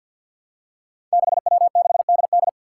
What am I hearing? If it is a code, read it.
Morse code, heard as a keyed tone, about 51 wpm, 700 Hz. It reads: 6W6DD
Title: Five Challenging Letter-Number Combinations 50wpm